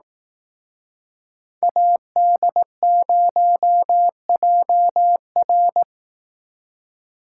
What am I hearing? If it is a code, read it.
Morse code, heard as a keyed tone, about 18 wpm, 700 Hz. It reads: AD0JR
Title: Speed Racing - USA 2x2 Call Signs 15wpm